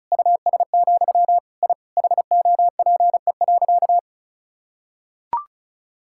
Morse code, heard as a keyed tone, about 36 wpm, 700 Hz. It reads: US, I HOPE.
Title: Sentences from Top 1000 Words - Review Part 6 - 35wpm